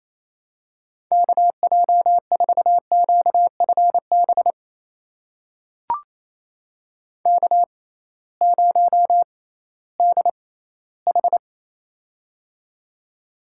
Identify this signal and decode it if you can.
Morse code, heard as a keyed tone, about 28 wpm, spaced as for 16 wpm, 700 Hz. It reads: KJ4QFB K0DH
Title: USA Call Signs - 6x Character Spacing - 28wpm